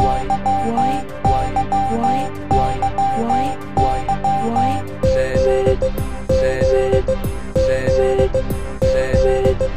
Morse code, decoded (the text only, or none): none